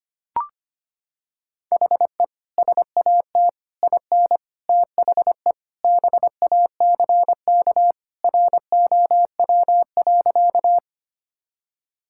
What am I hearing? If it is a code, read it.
Morse code, heard as a keyed tone, about 25 wpm, 700 Hz. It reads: HE SAT IN THE BACK ROW.
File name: practice / 25wpm / Sentences with 3-Word Phrases - 25wpm